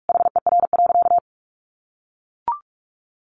Morse code, heard as a keyed tone, about 44 wpm, 700 Hz. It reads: HER.
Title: Sentences from Top 1200 Words 45wpm